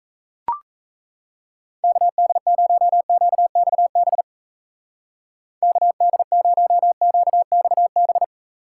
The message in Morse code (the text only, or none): KD0QXB KD0QXB